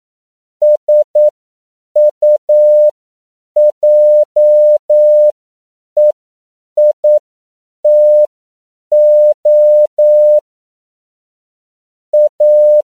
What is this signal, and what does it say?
Morse code, heard as a keyed tone, about 9 wpm, spaced as for 7 wpm, 600 Hz. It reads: SUJEITO A
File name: CW / 10PPM